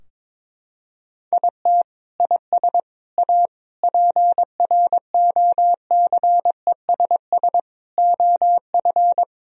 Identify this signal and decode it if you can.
Morse code, heard as a keyed tone, about 22 wpm, 700 Hz. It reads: IT IS A PROCESS OF
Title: Sentences from Top 1000 Words - Review Part 4 - 22wpm